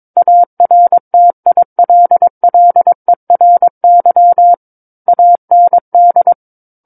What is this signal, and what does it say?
Morse code, heard as a keyed tone, about 22 wpm, 700 Hz. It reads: ARTILLERY AND
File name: Book One / 13-Chapter 13 - 22 WPM